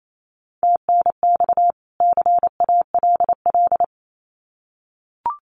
Morse code, heard as a keyed tone, about 28 wpm, 700 Hz. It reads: TNX CALL
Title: Common QSO Phrases 28wpm